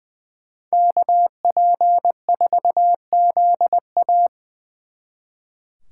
Morse code, heard as a keyed tone, about 24 wpm, 700 Hz. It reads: KP4ZA
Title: Speed Racing - USA 2x2 Call Signs 20wpm